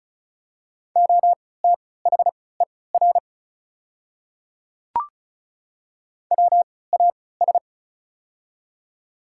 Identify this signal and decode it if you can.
Morse code, heard as a keyed tone, about 36 wpm, spaced as for 20 wpm, 700 Hz. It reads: OTHER WAS
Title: 3x Character Spacing - Top 100 Words - 35wpm